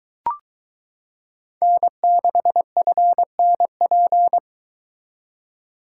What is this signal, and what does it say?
Morse code, heard as a keyed tone, about 22 wpm, 700 Hz. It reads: N6FNP